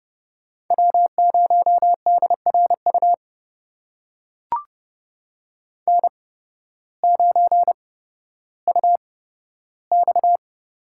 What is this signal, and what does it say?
Morse code, heard as a keyed tone, about 29 wpm, spaced as for 14 wpm, 700 Hz. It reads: W0DRU N9UX
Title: USA Call Signs - 8x Character Spacing - 30wpm